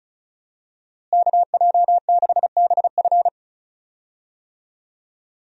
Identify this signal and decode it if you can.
Morse code, heard as a keyed tone, about 35 wpm, 700 Hz. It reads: KJ6BF
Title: USA Call Signs - 5 Characters - Repeated 2x - 35wpm